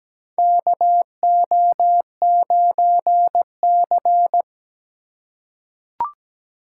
Morse code, heard as a keyed tone, about 17 wpm, 700 Hz. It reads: KO9C
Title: USA Call Signs - 4 Characters - 2x Character Spacing - 17wpm